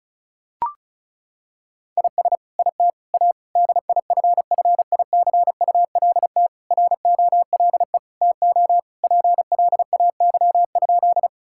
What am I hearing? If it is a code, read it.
Morse code, heard as a keyed tone, about 34 wpm, 700 Hz. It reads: IS IT A DIFFICULT ROLE TO PLAY?
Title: Sentences from Top 1000 Words - Review Part 6 - 35wpm